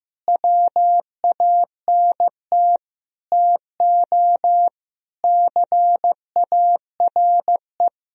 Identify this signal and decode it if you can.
Morse code, heard as a keyed tone, about 15 wpm, 700 Hz. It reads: WANT TO CARE
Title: Sentences from Top 1000 Words - Review Part 2 - 15wpm